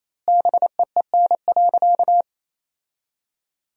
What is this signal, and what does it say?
Morse code, heard as a keyed tone, about 28 wpm, 700 Hz. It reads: BEEN.
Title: 3x Word-Spacing - Sentences from Top 300 Words - 28wpm